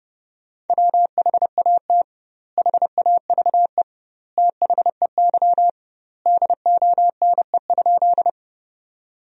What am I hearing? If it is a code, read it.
Morse code, heard as a keyed tone, about 32 wpm, 700 Hz. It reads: WHAT HAVE THEY DONE?
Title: 2x Word-Spacing - Sentences from Top 500 Words - 30wpm